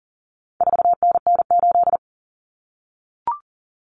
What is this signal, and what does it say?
Morse code, heard as a keyed tone, about 40 wpm, 700 Hz. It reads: 4NN8
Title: Common Cut Numbers - 4-Digit 40wpm